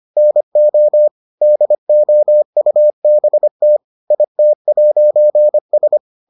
Morse code, heard as a keyed tone, about 25 wpm, 600 Hz. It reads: NO DOUBT IT'S